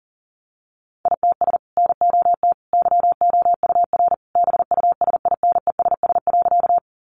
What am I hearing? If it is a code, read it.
Morse code, heard as a keyed tone, about 38 wpm, 700 Hz. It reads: ITS NOT YOUR BUSINESS.
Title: Sentences from Top 600 Words 40wpm